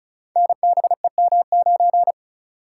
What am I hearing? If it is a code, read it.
Morse code, heard as a keyed tone, about 36 wpm, 700 Hz. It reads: NBEM9
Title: Code Groups - 5 Letter:Numbers - 35wpm